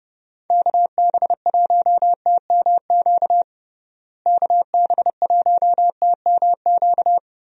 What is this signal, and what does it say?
Morse code, heard as a keyed tone, about 30 wpm, 700 Hz. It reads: KB1TMQ KB1TMQ
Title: USA Call Signs - 6 Characters - Repeated 2x - 30wpm